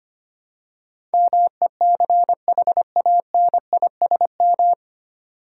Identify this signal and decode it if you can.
Morse code, heard as a keyed tone, about 25 wpm, 700 Hz. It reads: MECHANISM
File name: Nine Letter Words - 25wpm